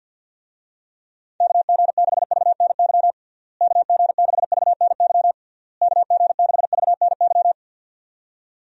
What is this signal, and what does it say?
Morse code, heard as a keyed tone, about 49 wpm, 700 Hz. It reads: KG6VNY KG6VNY KG6VNY
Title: USA Call Signs - 6 Characters - Repeated 3x - 50wpm